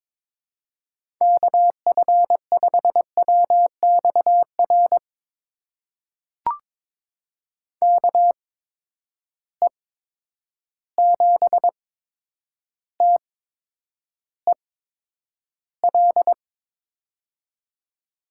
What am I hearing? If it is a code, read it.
Morse code, heard as a keyed tone, about 22 wpm, spaced as for 9 wpm, 700 Hz. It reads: KF5WXR KE7TEL
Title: USA Call Signs - 6 Characters - 8x Character Spacing - 22wpm